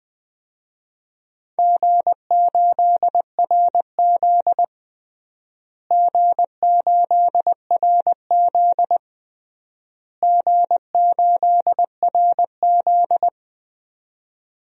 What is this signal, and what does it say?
Morse code, heard as a keyed tone, about 20 wpm, 700 Hz. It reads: G8RZ G8RZ G8RZ
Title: DXCC-100 Call Signs - 4 Characters - Repeated 3x - 20wpm